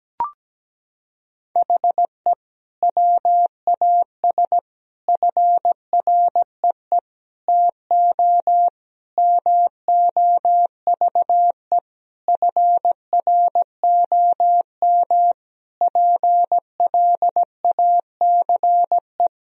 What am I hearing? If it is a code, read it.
Morse code, heard as a keyed tone, about 17 wpm, 700 Hz. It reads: HE WAS FREE TO MOVE FROM PLACE
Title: Sentences from Top 1000 Words - Review Part 1 - 17wpm